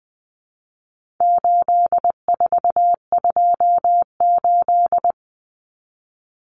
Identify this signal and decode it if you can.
Morse code, heard as a keyed tone, about 20 wpm, 700 Hz. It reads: 8428
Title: Numbers - 4-Digit 20wpm